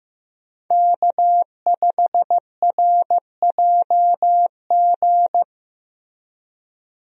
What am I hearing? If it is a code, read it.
Morse code, heard as a keyed tone, about 16 wpm, 700 Hz. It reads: K5RJG
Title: USA Call Signs - 5 Characters - Repeated 3x - 15wpm